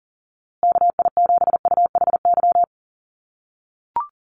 Morse code, heard as a keyed tone, about 40 wpm, 700 Hz. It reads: KI7UHY
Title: USA Call Signs Repeated 2x 40wpm